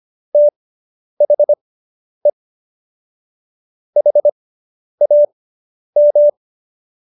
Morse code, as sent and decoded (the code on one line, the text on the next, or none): - .... / .... .- --
THE HAM